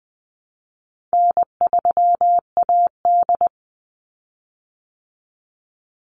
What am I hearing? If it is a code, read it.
Morse code, heard as a keyed tone, about 20 wpm, 700 Hz. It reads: N3AD
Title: Call Signs Repeated 3x - 20wpm